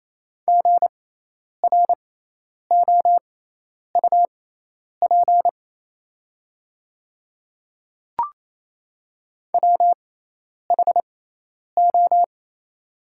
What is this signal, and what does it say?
Morse code, heard as a keyed tone, about 28 wpm, spaced as for 10 wpm, 700 Hz. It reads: GROUP WHO
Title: 6x Character Spacing - Top 500 Words - 28wpm